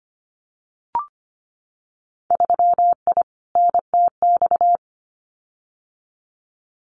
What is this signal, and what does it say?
Morse code, heard as a keyed tone, about 25 wpm, 700 Hz. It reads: 3I NTX
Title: Winter Field Day Exchanges 25wpm